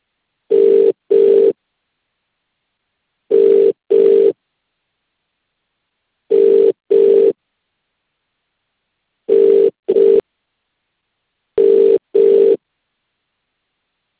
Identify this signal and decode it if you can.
Ringback tone (UK), 400 + 450 Hz.